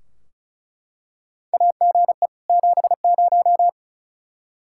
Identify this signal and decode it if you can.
Morse code, heard as a keyed tone, about 36 wpm, 700 Hz. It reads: AGE 70
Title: Practice QSO - Exchange 3 - More Info - 35wpm